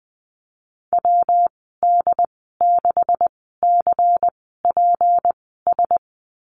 Morse code, heard as a keyed tone, about 20 wpm, spaced as for 14 wpm, 700 Hz. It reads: WD6CPS
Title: 193 Lesson 31g - Letter v - Call Signs 20wpm